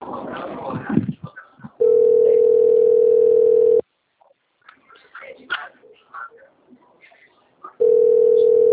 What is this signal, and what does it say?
Ringback tone (North America), 440 + 480 Hz.